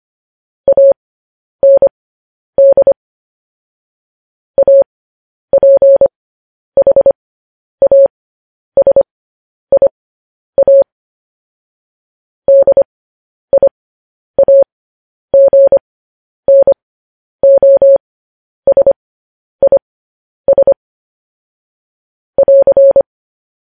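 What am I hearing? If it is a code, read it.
Morse code, heard as a keyed tone, about 25 wpm, spaced as for 10 wpm, 550 Hz. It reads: AND APHASIA DIAGNOSIS <AR>
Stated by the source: Morse code news headlines transmission.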